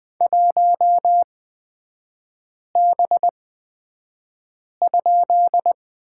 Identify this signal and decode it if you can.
Morse code, heard as a keyed tone, about 20 wpm, spaced as for 5 wpm, 700 Hz. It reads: 1B?
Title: 229 Lesson 37e - Letter x - Three Characters 20wpm